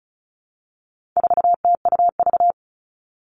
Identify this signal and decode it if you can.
Morse code, heard as a keyed tone, about 34 wpm, 700 Hz. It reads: 4TUV
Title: All Cut Numbers - 4-Digit 35wpm